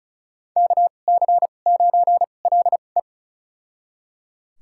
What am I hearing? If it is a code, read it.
Morse code, heard as a keyed tone, about 34 wpm, spaced as for 25 wpm, 700 Hz. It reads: KC9LE
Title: USA Call Signs - 5 Characters - 2x Character Spacing - 35wpm